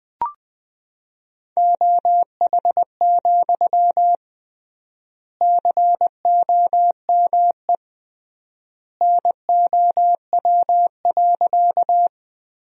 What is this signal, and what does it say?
Morse code, heard as a keyed tone, about 20 wpm, 700 Hz. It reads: OH, COME NOW.
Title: 3x Word-Spacing - Sentences from Top 500 Words - 20wpm